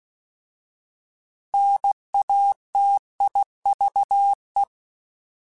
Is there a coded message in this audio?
Yes: Morse code from 1.5 to 4.6 s, heard as a keyed tone, about 16 wpm, 800 Hz. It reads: NATIVE